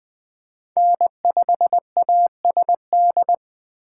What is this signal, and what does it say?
Morse code, heard as a keyed tone, about 21 wpm, 700 Hz. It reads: N5ASD
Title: Speed Racing - USA 1x3 Call Signs 17wpm